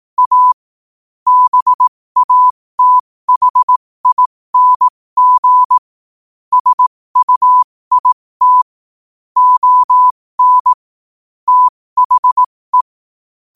Morse code, heard as a keyed tone, about 18 wpm, spaced as for 15 wpm, 1000 Hz. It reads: A BATHING SUIT ON THE
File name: qotdfiles / 15wpm